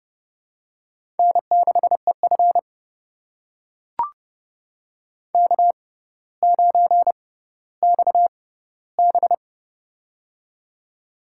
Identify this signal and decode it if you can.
Morse code, heard as a keyed tone, about 30 wpm, spaced as for 15 wpm, 700 Hz. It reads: N6EF K9XB
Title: USA Call Signs - 4 Characters - 6x Character Spacing - 30wpm